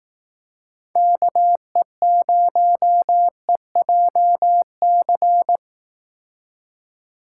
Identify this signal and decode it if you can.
Morse code, heard as a keyed tone, about 18 wpm, 700 Hz. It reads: KE0EJC
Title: Speed Racing - USA Call Signs 15wpm